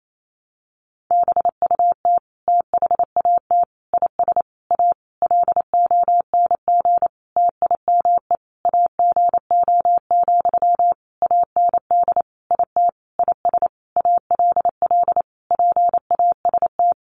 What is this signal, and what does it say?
Morse code, heard as a keyed tone, about 28 wpm, 700 Hz. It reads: BUT THAT IS A LONG TIME AGO, AND IT IS ALL PAST
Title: Sentences from Top 600 Words 28wpm